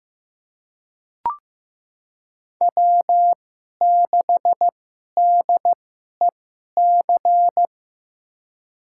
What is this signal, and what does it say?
Morse code, heard as a keyed tone, about 15 wpm, spaced as for 11 wpm, 700 Hz. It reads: W6DEC
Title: USA Call Signs - 5 Characters - 2x Character Spacing - 15wpm